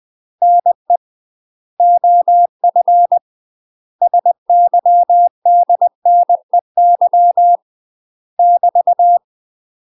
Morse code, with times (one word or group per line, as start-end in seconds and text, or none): 0.4-1.0 NE
1.8-3.2 OF
4.0-7.6 SYDNEY
8.4-9.2 <BT>